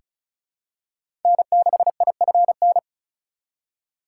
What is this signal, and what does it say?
Morse code, heard as a keyed tone, about 34 wpm, 700 Hz. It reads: N6IFN
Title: USA Call Signs - 2x Character Spacing - 35wpm